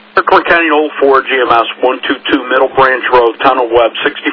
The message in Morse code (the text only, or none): none